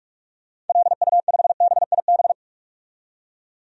Morse code, heard as a keyed tone, about 41 wpm, 700 Hz. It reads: LU5BIB
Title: DXCC-100 Call Signs - 6 Characters - 45wpm